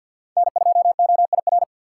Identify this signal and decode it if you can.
Morse code, heard as a keyed tone, about 53 wpm, 700 Hz. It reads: N2OIR
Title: USA Call Signs - 5 Characters - 6x Character Spacing - 50wpm